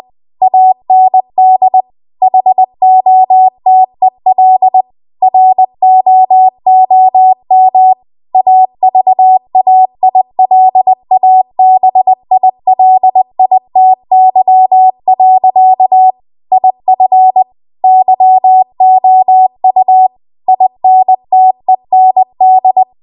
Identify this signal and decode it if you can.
Morse code, heard as a keyed tone, about 20 wpm, 750 Hz. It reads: AND HOTEL ROOM AVAILABILITY. IF YOU INTEND